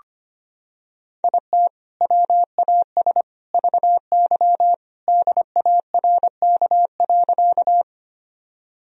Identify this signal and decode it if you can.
Morse code, heard as a keyed tone, about 25 wpm, 700 Hz. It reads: IT WAS VY DARK.
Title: Sentences from Top 500 Words and Abbreviations 25wpm